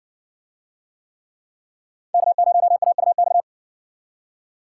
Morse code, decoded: K0AUX

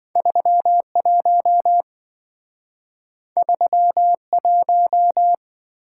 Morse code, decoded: 31 31

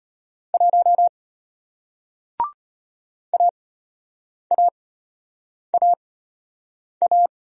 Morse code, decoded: 1 AAAA

